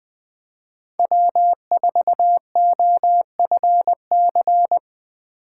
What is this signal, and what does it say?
Morse code, heard as a keyed tone, about 20 wpm, 700 Hz. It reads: W4OFC
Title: USA Call Signs - 6x Character Spacing - 20wpm